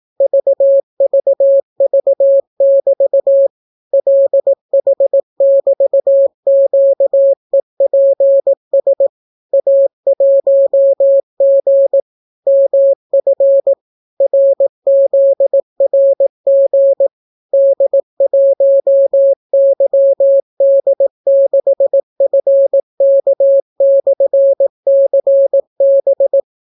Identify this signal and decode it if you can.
Morse code, heard as a keyed tone, about 18 wpm, 550 Hz. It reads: VVV<BT> LH<BT>QEPS A1G MF RZRG D1YD6FK/CB